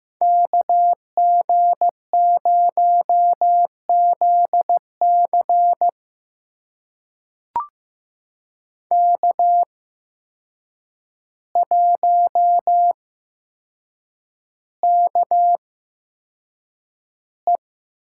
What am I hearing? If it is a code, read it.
Morse code, heard as a keyed tone, about 15 wpm, spaced as for 7 wpm, 700 Hz. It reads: KG0ZC K1KE
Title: USA Call Signs - 5 Characters - 8x Character Spacing - 15wpm